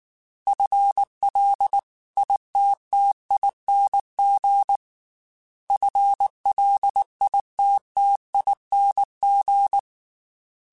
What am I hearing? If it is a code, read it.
Morse code, heard as a keyed tone, about 19 wpm, 800 Hz. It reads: FLITTING FLITTING
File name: FL / FL mp3s / FL Prefixes 3x2 sp @18